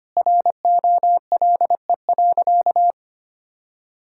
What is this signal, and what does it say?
Morse code, heard as a keyed tone, about 25 wpm, 700 Hz. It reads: ROLE.
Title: Sentences from Top 1000 Words - Review Part 6 - 25wpm